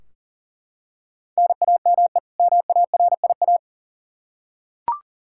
Morse code, heard as a keyed tone, about 40 wpm, 700 Hz. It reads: NAME MARIA
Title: Practice QSO - Exchange 1 - Beginning - 40wpm